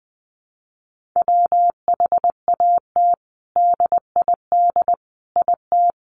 Morse code, heard as a keyed tone, about 20 wpm, 700 Hz. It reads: WHAT DID IT
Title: Sentences from Top 1400 Words 20wpm